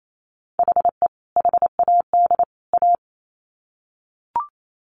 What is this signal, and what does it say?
Morse code, heard as a keyed tone, about 28 wpm, 700 Hz. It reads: HE HAD A
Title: Sets of 3 Words 28wpm